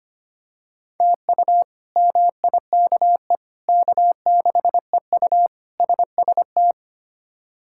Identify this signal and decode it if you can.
Morse code, heard as a keyed tone, about 25 wpm, 700 Hz. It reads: TU MIKE K6EU SST
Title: K1USN SST QSO Practice - 25wpm